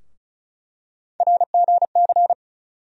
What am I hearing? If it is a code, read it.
Morse code, heard as a keyed tone, about 36 wpm, 700 Hz. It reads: RGC
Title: USA Call Sign Postfixes - 3 Letters 35wpm